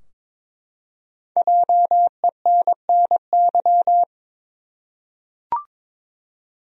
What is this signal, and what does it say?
Morse code, heard as a keyed tone, about 23 wpm, 700 Hz. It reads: JENNY
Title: USA Common Names - 22wpm